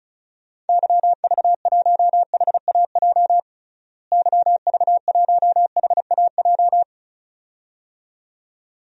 Morse code, heard as a keyed tone, about 36 wpm, 700 Hz. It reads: YV1HAJ YV1HAJ
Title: DXCC-100 Call Signs - 6 Characters - Repeated 2x - 35wpm